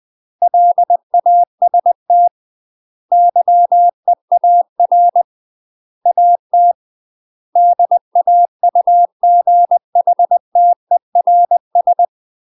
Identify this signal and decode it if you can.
Morse code, heard as a keyed tone, about 20 wpm, 700 Hz. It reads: LAST YEAR AT DAUGHTERS
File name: UK MASTER QSOs TRAINING / 20WPM / UK MASTER QSOs 0058